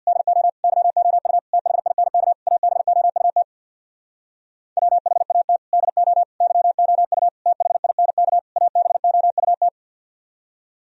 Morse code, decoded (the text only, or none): DO YOU THINK ABOUT WHAT DO YOU THINK ABOUT